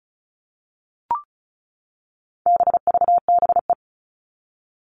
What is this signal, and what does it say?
Morse code, heard as a keyed tone, about 34 wpm, 700 Hz. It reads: BVBE
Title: All Cut Numbers - 4-Digit 35wpm